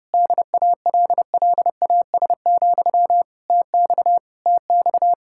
Morse code, heard as a keyed tone, about 32 wpm, 700 Hz. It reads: DALLAS, TX TX